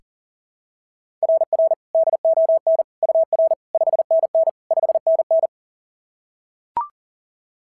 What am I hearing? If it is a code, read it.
Morse code, heard as a keyed tone, about 39 wpm, 690 Hz. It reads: RR DON UR 5NN 5NN
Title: SOTA QSO Practice 40wpm